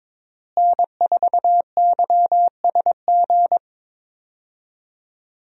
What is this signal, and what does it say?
Morse code, heard as a keyed tone, about 22 wpm, 700 Hz. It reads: N4YSG
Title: USA Call Signs - 5 Characters - Repeated 3x - 22wpm